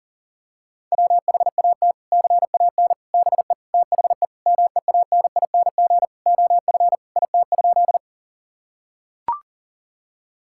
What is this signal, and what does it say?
Morse code, heard as a keyed tone, about 40 wpm, 700 Hz. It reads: WHAT CAN BE THE MEANING OF IT?
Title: Sentences from Top 1000 Words - Review Part 6 - 40wpm